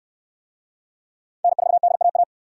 Morse code, heard as a keyed tone, about 59 wpm, 700 Hz. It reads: N4DAA